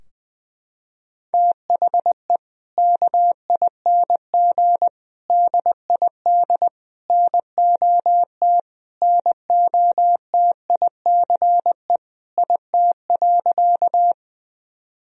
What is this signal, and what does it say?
Morse code, heard as a keyed tone, about 20 wpm, 700 Hz. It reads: THE KING DID NOT NOTICE IT.
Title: Sentences from Top 1000 Words - Review Part 2 - 20wpm